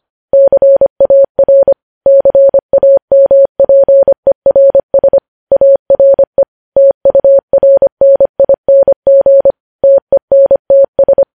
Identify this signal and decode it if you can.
Morse code, heard as a keyed tone, about 25 wpm, 550 Hz. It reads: CAR CAMPERS ARE TURNING TENTS
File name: podcast / News headlines-1x-25x25